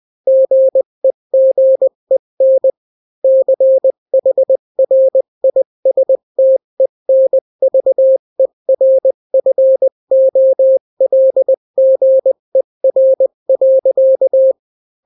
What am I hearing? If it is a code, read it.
Morse code, heard as a keyed tone, about 20 wpm, 530 Hz. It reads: GEGEN CHRISTENVERFOLGER.
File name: daily / 18wpm